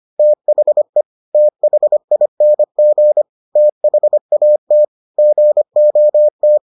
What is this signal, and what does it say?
Morse code, heard as a keyed tone, about 25 wpm, 600 Hz. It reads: THE THING THAT GOT